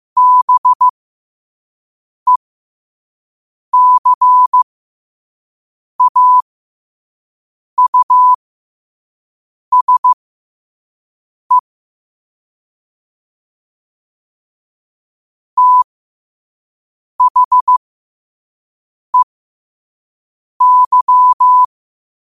Morse code, heard as a keyed tone, about 15 wpm, spaced as for 5 wpm, 1000 Hz. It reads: BECAUSE THEY
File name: qotdfiles / 5wpm